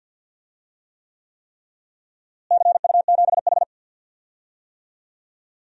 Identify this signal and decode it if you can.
Morse code, heard as a keyed tone, about 52 wpm, 700 Hz. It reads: CU7H
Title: DXCC-100 Call Signs - 4 Characters - Repeated 3x - 50wpm